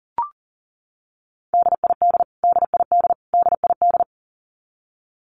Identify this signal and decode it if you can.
Morse code, heard as a keyed tone, about 40 wpm, 700 Hz. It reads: DID DID DID
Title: Top 500 Words Repeated 40wpm